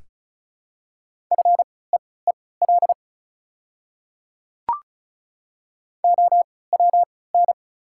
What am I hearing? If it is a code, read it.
Morse code, heard as a keyed tone, about 34 wpm, spaced as for 20 wpm, 690 Hz. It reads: FEEL OWN
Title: 3x Character Spacing - Top 500 Words - 35wpm